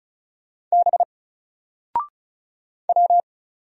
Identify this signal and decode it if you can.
Morse code, heard as a keyed tone, about 34 wpm, 700 Hz. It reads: B W
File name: Single-Letter-Number-35wpm